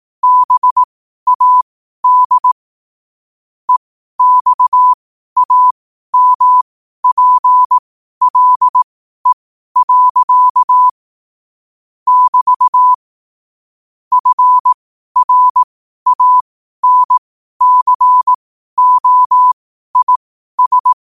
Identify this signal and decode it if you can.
Morse code, heard as a keyed tone, about 18 wpm, spaced as for 12 wpm, 1000 Hz. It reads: BAD EXAMPLE. <BT> FRANCOIS
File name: qotdfiles / 12wpm